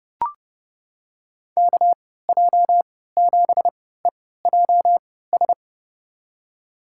Morse code, heard as a keyed tone, about 32 wpm, spaced as for 17 wpm, 700 Hz. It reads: KJ7EJS
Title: USA Call Signs - 6 Characters - 3x Character Spacing - 30wpm